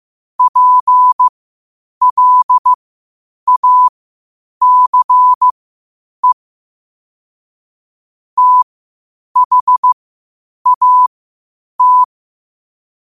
Morse code, heard as a keyed tone, about 15 wpm, spaced as for 8 wpm, 1000 Hz. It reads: PLACE THAT
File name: qotdfiles / 8wpm